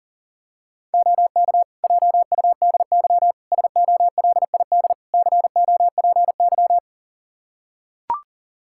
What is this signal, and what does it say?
Morse code, heard as a keyed tone, about 40 wpm, 700 Hz. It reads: OK JUDY SOLID COPY
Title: Practice QSO - Exchange 3 - More Info - 40wpm